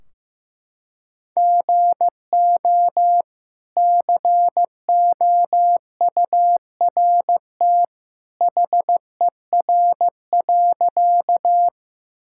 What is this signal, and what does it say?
Morse code, heard as a keyed tone, about 16 wpm, 700 Hz. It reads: GO COURT HER.
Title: Sentences from Top 1600 Words 15wpm